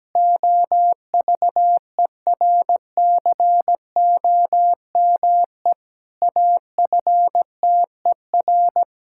Morse code, heard as a keyed tone, about 17 wpm, 700 Hz. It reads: OVERCOME AFTER
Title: Pangram Challenge v2 - 17wpm